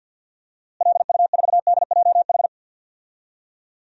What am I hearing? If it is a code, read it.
Morse code, heard as a keyed tone, about 50 wpm, 700 Hz. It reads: LU4DJH
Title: DXCC-100 Call Signs - 6 Characters - Repeated 2x - 50wpm